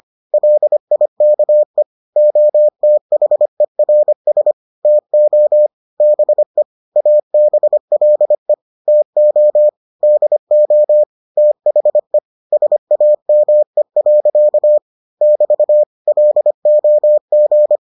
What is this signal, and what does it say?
Morse code, heard as a keyed tone, about 25 wpm, 600 Hz. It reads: LIKE OTHERS TO BE ABLE TO DO THE SAME. <BT> LOG